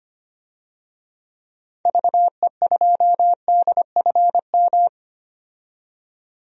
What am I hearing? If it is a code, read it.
Morse code, heard as a keyed tone, about 25 wpm, 700 Hz. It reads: VE2DFM
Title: DXCC-100 Call Signs - 6 Characters - 25wpm